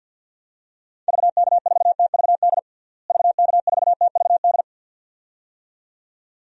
Morse code, heard as a keyed tone, about 51 wpm, 700 Hz. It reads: VK4TVD VK4TVD
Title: DXCC-100 Call Signs - 6 Characters - Repeated 2x - 50wpm